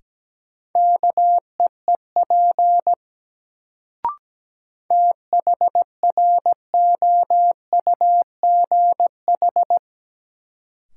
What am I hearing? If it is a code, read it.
Morse code, heard as a keyed tone, about 17 wpm, 700 Hz. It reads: KEEP THROUGH
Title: Sets of 2 Words - New Word Drill - 17wpm